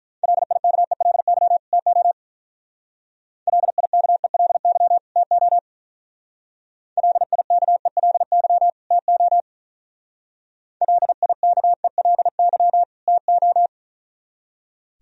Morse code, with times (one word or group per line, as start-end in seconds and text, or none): 0.2-1.6 LIKELY
1.7-2.1 TO
3.5-5.0 LIKELY
5.2-5.6 TO
7.0-8.7 LIKELY
8.9-9.4 TO
10.8-12.8 LIKELY
13.1-13.7 TO